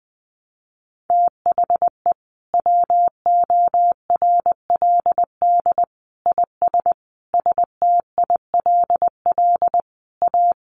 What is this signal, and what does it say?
Morse code, heard as a keyed tone, about 20 wpm, 700 Hz. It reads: THE WORLD IS STILL A
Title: Sentences from Top 1200 Words 20wpm